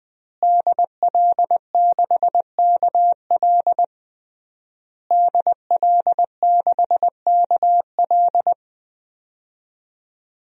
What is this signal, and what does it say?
Morse code, heard as a keyed tone, about 20 wpm, 700 Hz. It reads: DL6KL DL6KL
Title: DXCC-100 Call Signs - 5 Characters - Repeated 3x - 20wpm